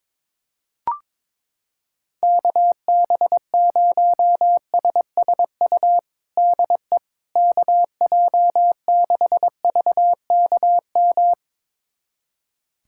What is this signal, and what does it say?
Morse code, heard as a keyed tone, about 22 wpm, 700 Hz. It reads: KB0SSU DE KJ6VKM
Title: USA Callsign DE Callsign - 6 Characters - 22wpm